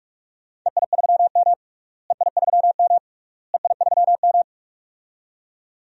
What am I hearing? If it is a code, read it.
Morse code, heard as a keyed tone, about 46 wpm, 690 Hz. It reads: EI3M EI3M EI3M
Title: DXCC-100 Call Signs - 4 Characters - Repeated 3x - 45wpm